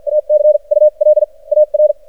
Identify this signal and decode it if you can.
Morse code, heard as a keyed tone, about 53 wpm, spaced as for 36 wpm, 610 Hz. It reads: ACULAR
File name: archived / CRSnet